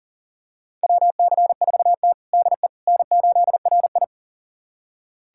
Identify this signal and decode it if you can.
Morse code, heard as a keyed tone, about 37 wpm, 700 Hz. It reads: WC4T DE N8RI
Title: USA Callsign DE Callsign - 4 Characters - 40wpm